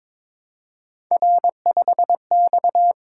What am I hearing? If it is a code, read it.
Morse code, heard as a keyed tone, about 22 wpm, 700 Hz. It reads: R5X
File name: practice / 22wpm / Three Letter-Number Combinations - 22wpm